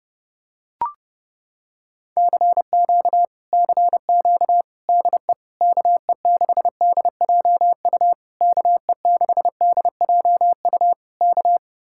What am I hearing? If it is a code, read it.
Morse code, heard as a keyed tone, about 28 wpm, 700 Hz. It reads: CQ CQ DE KE6DJU KE6DJU K